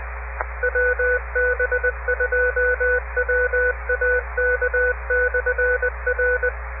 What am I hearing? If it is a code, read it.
Morse code, heard as a keyed tone, about 20 wpm, 500 Hz, with radio noise. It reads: WB2WAK/R